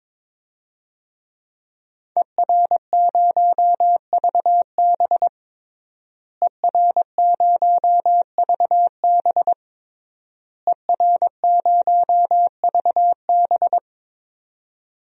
Morse code, heard as a keyed tone, about 22 wpm, 700 Hz. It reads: ER0VB ER0VB ER0VB
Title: DXCC-100 Call Signs - 5 Characters - Repeated 3x - 22wpm